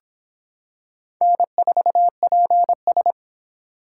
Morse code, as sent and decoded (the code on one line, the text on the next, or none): -. ....- .--. ...
N4PS